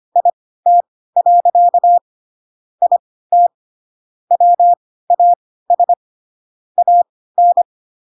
Morse code, heard as a keyed tone, about 25 wpm, spaced as for 16 wpm, 700 Hz. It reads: IT. IT WAS AN